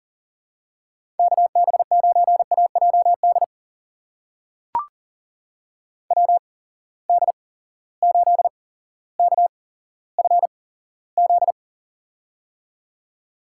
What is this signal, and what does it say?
Morse code, heard as a keyed tone, about 38 wpm, spaced as for 17 wpm, 700 Hz. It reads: KB9AJD WD8KFZ
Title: USA Call Signs - 6 Characters - 8x Character Spacing - 40wpm